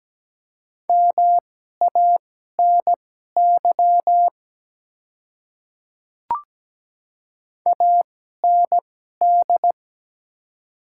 Morse code, heard as a keyed tone, about 17 wpm, spaced as for 12 wpm, 700 Hz. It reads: MANY AND
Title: 2x Character Spacing - Top 200 Words - 17wpm